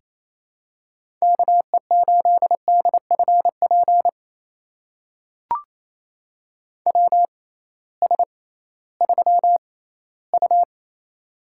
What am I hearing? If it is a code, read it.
Morse code, heard as a keyed tone, about 28 wpm, spaced as for 16 wpm, 700 Hz. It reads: KE8DFP WS3U
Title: USA Call Signs - 6x Character Spacing - 28wpm